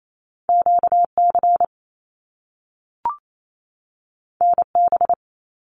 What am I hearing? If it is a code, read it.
Morse code, heard as a keyed tone, about 28 wpm, 700 Hz. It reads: QC NB
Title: Canadian Provinces and Territories 28wpm